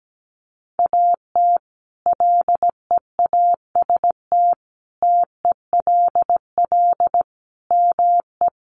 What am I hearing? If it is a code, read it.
Morse code, heard as a keyed tone, about 17 wpm, 700 Hz. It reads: AT LEAST TELL ME
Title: Sentences from Top 700 Words 17wpm